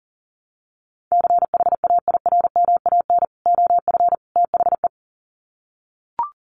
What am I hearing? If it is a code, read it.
Morse code, heard as a keyed tone, about 40 wpm, 700 Hz. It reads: CHAIRMAN OF THE
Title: Sets of 3 Words 40wpm